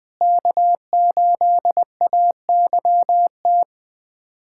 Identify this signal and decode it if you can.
Morse code, heard as a keyed tone, about 20 wpm, 700 Hz. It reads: K8AYT
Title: USA Callsign DE Callsign 20wpm